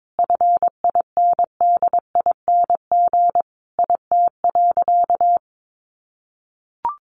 Morse code, heard as a keyed tone, about 22 wpm, 700 Hz. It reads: FINDING IT.
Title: Sentences from Top 1100 Words 22wpm